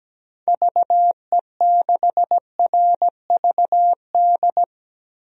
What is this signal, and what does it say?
Morse code, heard as a keyed tone, about 17 wpm, 700 Hz. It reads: VE6RVD
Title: DXCC-100 Call Signs - 6 Characters - Repeated 3x - 17wpm